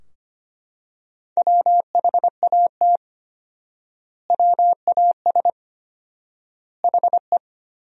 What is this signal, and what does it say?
Morse code, heard as a keyed tone, about 25 wpm, 700 Hz. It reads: WHAT WAS HE
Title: 4x Word-Spacing - Sentences from Top 100 Words - 25wpm